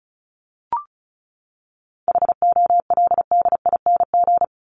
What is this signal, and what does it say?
Morse code, heard as a keyed tone, about 34 wpm, 700 Hz. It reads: HOLDING
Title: ING Suffix - Top 100 Words with Base Words 35wpm